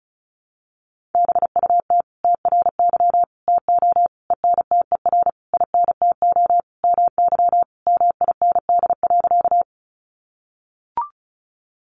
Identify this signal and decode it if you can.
Morse code, heard as a keyed tone, about 34 wpm, 700 Hz. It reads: BUT TRY TO ENTER INTO MY MIND.
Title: Sentences from Top 1300 Words 35wpm